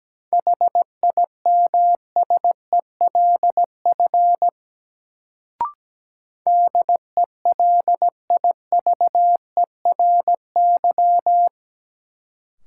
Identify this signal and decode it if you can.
Morse code, heard as a keyed tone, about 17 wpm, 700 Hz. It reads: HIMSELF DELIVERY
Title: Sentences from Top 1500 Words - New Word Drill - 17wpm